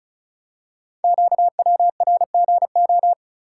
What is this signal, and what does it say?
Morse code, heard as a keyed tone, about 36 wpm, 700 Hz. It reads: QWRGO
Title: Code Groups - 5 Letters - 35wpm